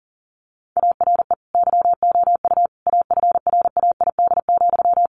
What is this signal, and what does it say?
Morse code, heard as a keyed tone, about 40 wpm, 700 Hz. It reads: ARE YOU AFRAID,
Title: Sentences from Top 1500 Words 40wpm